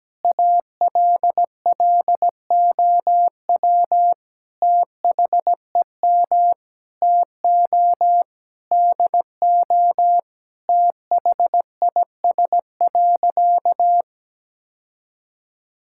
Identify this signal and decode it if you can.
Morse code, heard as a keyed tone, about 17 wpm, 700 Hz. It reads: ALLOW THEM TO DO THIS.
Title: Sentences from Top 1000 Words - Review Part 5 - 17wpm